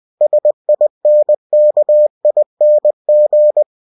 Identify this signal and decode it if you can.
Morse code, heard as a keyed tone, about 20 wpm, 600 Hz. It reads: SINKING
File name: cw-books / winnie-the-pooh / winnie-20wpm-page-0038